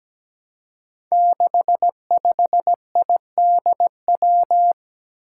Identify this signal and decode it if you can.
Morse code, heard as a keyed tone, about 17 wpm, 700 Hz. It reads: 65IDW
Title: Five Challenging Letter-Number Combinations 17wpm